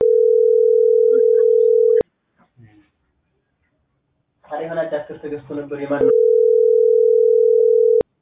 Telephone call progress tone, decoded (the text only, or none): ringback tone